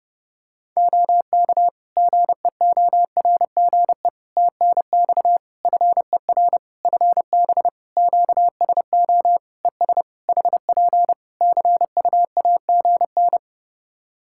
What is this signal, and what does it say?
Morse code, heard as a keyed tone, about 30 wpm, 700 Hz. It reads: OK GEORGE TNX FER FB QSO ES HP CUAGN
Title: K1USN SST QSO Practice 30wpm